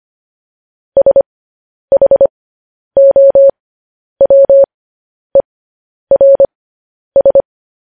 Morse code, heard as a keyed tone, about 25 wpm, spaced as for 10 wpm, 550 Hz. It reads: SHOWERS
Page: news headlines in morse code at 10 wpm